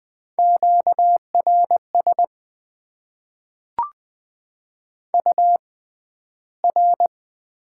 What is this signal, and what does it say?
Morse code, heard as a keyed tone, about 20 wpm, spaced as for 12 wpm, 700 Hz. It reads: QRS UR
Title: Top 90 QSO Elements - 6x Character Spacing - 20wpm